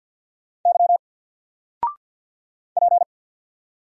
Morse code, heard as a keyed tone, about 50 wpm, 700 Hz. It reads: Y P
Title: Single Letter-Number 50wpm